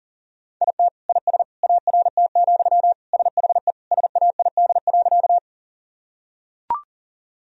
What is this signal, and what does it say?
Morse code, heard as a keyed tone, about 37 wpm, 700 Hz. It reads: IT IS ART, SHE SAID.
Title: Sentences from Top 1000 Words - Review Part 3 - 40wpm